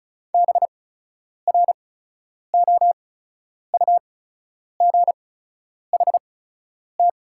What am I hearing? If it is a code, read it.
Morse code, heard as a keyed tone, about 37 wpm, spaced as for 10 wpm, 690 Hz. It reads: BROUGHT
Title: 8x Character Spacing - Top 500 Words - 35wpm